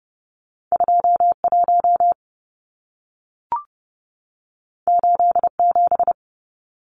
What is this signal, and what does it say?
Morse code, heard as a keyed tone, about 30 wpm, 700 Hz. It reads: 21 87
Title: Numbers - 2-Digit 30wpm